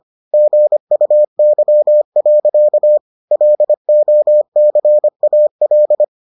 Morse code, heard as a keyed tone, about 25 wpm, 600 Hz. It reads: GUY. LOCAL